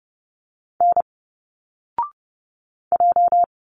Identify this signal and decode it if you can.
Morse code, heard as a keyed tone, about 30 wpm, 700 Hz. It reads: N J
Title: Single Letters 30wpm